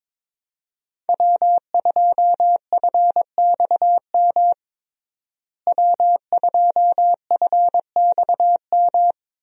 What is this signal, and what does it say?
Morse code, heard as a keyed tone, about 22 wpm, 700 Hz. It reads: W2FXM W2FXM
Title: USA Call Signs - 5 Characters - Repeated 2x - 22wpm